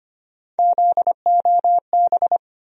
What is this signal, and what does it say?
Morse code, heard as a keyed tone, about 25 wpm, 700 Hz. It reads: ZOB
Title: USA Call Sign Postfixes - 3 Letters 25wpm